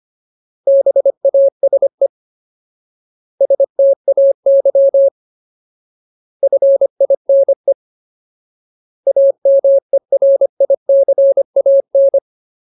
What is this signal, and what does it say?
Morse code, heard as a keyed tone, about 25 wpm, 550 Hz. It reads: BASE STAY FINE AMERICAN
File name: Ultimate Word List / 25WPM Ultimate